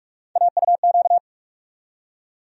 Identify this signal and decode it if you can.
Morse code, heard as a keyed tone, about 46 wpm, 700 Hz. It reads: AUQ
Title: USA Call Sign Postfixes - 3 Letters 45wpm